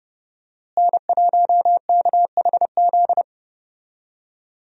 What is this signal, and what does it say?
Morse code, heard as a keyed tone, about 31 wpm, 700 Hz. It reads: N1KHZ
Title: USA Call Signs - 5 Characters - Repeated 2x - 30wpm